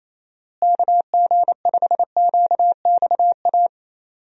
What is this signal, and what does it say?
Morse code, heard as a keyed tone, about 28 wpm, 700 Hz. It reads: KG5QXA
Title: USA Call Signs - 6 Characters - Repeated 3x - 28wpm